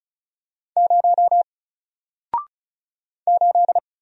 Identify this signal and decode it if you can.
Morse code, heard as a keyed tone, about 36 wpm, 700 Hz. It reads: 0 8
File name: Single-Letter-Number-35wpm